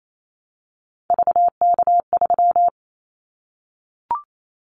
Morse code, heard as a keyed tone, about 28 wpm, 700 Hz. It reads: VK3
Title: Callsign Prefixes for Top 50 DXCC - 28wpm